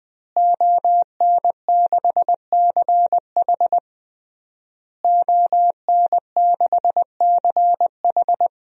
Morse code, heard as a keyed tone, about 21 wpm, 700 Hz. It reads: ON6CH ON6CH
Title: DXCC-100 Call Signs - 5 Characters - Repeated 3x - 20wpm